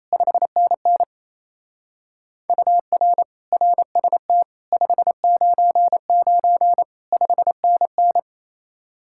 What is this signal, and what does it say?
Morse code, heard as a keyed tone, about 35 wpm, 700 Hz. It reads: 5NN UR RST 599 5NN
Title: Speed-Racing Common QSO Phrases - 28wpm